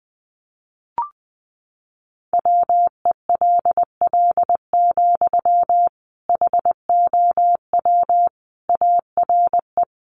Morse code, heard as a keyed tone, about 20 wpm, 700 Hz. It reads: WELL, HOW ARE
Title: Sentences from Top 200 Words 20wpm